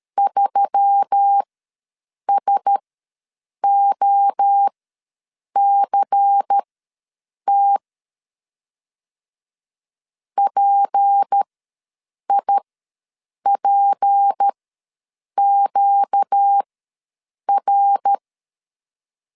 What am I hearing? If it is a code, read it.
Morse code, heard as a keyed tone, about 13 wpm, spaced as for 7 wpm, 800 Hz. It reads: 3SOCT PIPQR
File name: LEZIONI / 6 PKX0 / 6 P ws 12wpm 3117